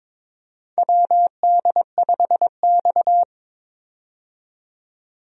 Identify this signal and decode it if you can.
Morse code, heard as a keyed tone, about 22 wpm, 700 Hz. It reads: WD5X
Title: USA Call Signs - 4 Characters - 22wpm